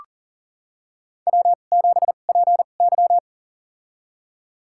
Morse code, heard as a keyed tone, about 40 wpm, spaced as for 29 wpm, 700 Hz. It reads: W7PY
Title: USA Call Signs - 4 Characters - 2x Character Spacing - 40wpm